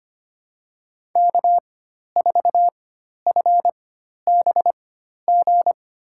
Morse code, heard as a keyed tone, about 25 wpm, spaced as for 12 wpm, 700 Hz. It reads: K4FBG
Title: USA Call Signs - 5 Characters - 4x Character Spacing - 25wpm